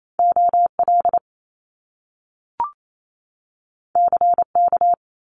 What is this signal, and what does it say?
Morse code, heard as a keyed tone, about 28 wpm, 700 Hz. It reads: OL CK
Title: Top Two Letter Combinations in Words 28wpm